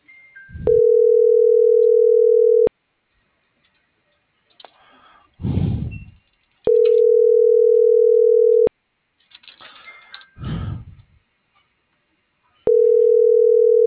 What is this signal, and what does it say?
Ringback tone (North America), 440 + 480 Hz.